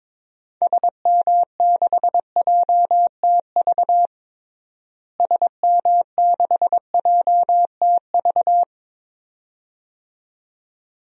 Morse code, heard as a keyed tone, about 22 wpm, 700 Hz. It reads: SM6JTV SM6JTV
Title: DXCC-100 Call Signs - 6 Characters - Repeated 2x - 22wpm